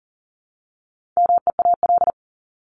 Morse code, heard as a keyed tone, about 37 wpm, 700 Hz. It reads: MEAL
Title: Top 1751-2000 Words 40wpm